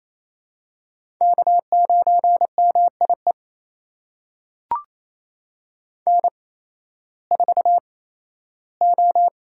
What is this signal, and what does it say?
Morse code, heard as a keyed tone, about 28 wpm, spaced as for 15 wpm, 700 Hz. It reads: K9MIE N4O
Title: USA Call Signs - 5 Characters - 8x Character Spacing - 28wpm